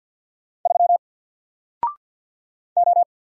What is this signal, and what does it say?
Morse code, heard as a keyed tone, about 50 wpm, 700 Hz. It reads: 3 O
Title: Single Letter-Number 50wpm